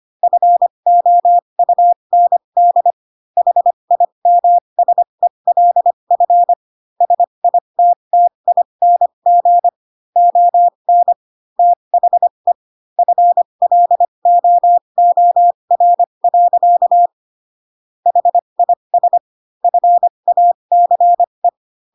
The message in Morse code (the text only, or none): FOUND HIMSELF SITTING ON THE FLOOR. HIS FACE